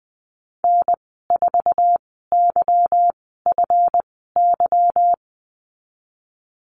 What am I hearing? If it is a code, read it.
Morse code, heard as a keyed tone, about 20 wpm, spaced as for 15 wpm, 700 Hz. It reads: N4YFY
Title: 154 Lesson 25g - Letter p - Call Signs 20wpm